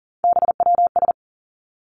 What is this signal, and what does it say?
Morse code, heard as a keyed tone, about 40 wpm, 700 Hz. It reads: BWS
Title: Three Challenging Letter-Number Combinations 40wpm